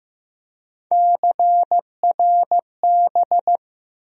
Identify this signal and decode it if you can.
Morse code, heard as a keyed tone, about 15 wpm, 700 Hz. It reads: CRB
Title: USA Call Sign Postfixes - 3 Letters 15wpm